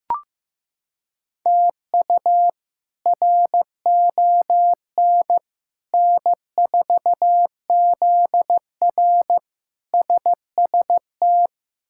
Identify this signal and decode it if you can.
Morse code, heard as a keyed tone, about 15 wpm, 700 Hz. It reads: TU RON N4ZR SST
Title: K1USN SST QSO Practice - 15wpm